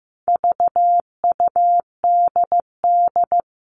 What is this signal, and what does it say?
Morse code, heard as a keyed tone, about 15 wpm, 700 Hz. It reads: VUDD
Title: All Cut Numbers - 4-Digit 15wpm